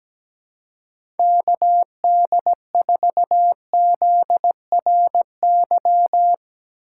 Morse code, heard as a keyed tone, about 17 wpm, 700 Hz. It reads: KD4ZRY
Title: USA Call Signs - 6 Characters - 8x Character Spacing - 17wpm